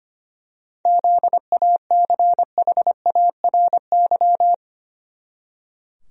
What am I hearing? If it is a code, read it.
Morse code, heard as a keyed tone, about 25 wpm, 700 Hz. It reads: ZACHARY
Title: USA Common Names - 25wpm